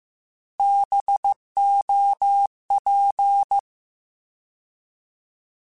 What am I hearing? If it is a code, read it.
Morse code, heard as a keyed tone, about 15 wpm, 800 Hz. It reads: BOP
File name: Alphabetical / Alphabetical mp3s / Bs @13